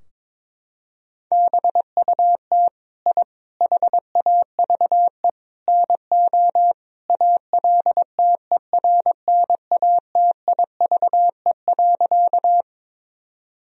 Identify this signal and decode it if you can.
Morse code, heard as a keyed tone, about 22 wpm, 700 Hz. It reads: BUT I HAVE NO ALTERNATIVE.
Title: Sentences from Top 1000 Words - Review Part 4 - 22wpm